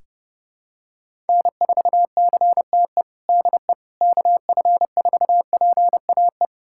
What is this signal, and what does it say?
Morse code, heard as a keyed tone, about 32 wpm, 700 Hz. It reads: N4CTE DE KF4PAE
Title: USA Callsign DE Callsign 30wpm